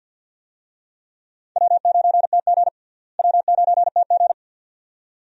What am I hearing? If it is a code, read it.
Morse code, heard as a keyed tone, about 49 wpm, 700 Hz. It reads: W9TG W9TG
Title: USA Call Signs - 4 Characters - Repeated 2x - 50wpm